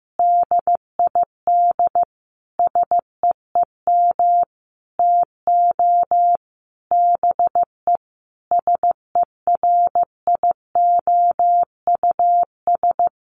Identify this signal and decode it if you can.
Morse code, heard as a keyed tone, about 15 wpm, 700 Hz. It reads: DID SEEM TO BE SERIOUS
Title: Sentences from Top 1100 Words 15wpm